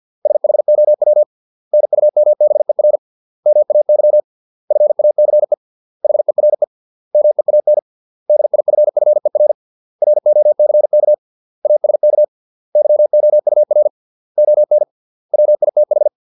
Morse code, heard as a keyed tone, about 50 wpm, 600 Hz. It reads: SHOW NUMBER MAY FACE HERE MEAN DIFFER ROCK ASK YOUR ON WITH